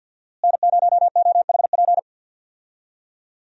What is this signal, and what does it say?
Morse code, heard as a keyed tone, about 49 wpm, 700 Hz. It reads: N0OHP
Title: USA Call Signs - 5 Characters - Repeated 3x - 50wpm